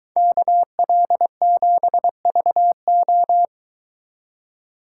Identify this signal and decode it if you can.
Morse code, heard as a keyed tone, about 22 wpm, 700 Hz. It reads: KL7VO